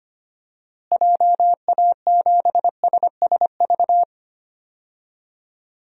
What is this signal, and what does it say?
Morse code, heard as a keyed tone, about 25 wpm, 700 Hz. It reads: JA7SSV
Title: DXCC-100 Call Signs - 6 Characters - Repeated 3x - 25wpm